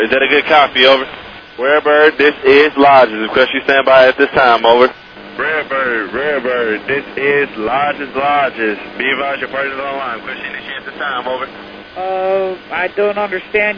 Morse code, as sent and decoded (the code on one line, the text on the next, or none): none